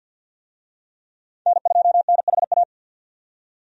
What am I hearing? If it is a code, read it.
Morse code, heard as a keyed tone, about 51 wpm, 700 Hz. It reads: N2NHA
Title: USA Call Signs - 5 Characters - 50wpm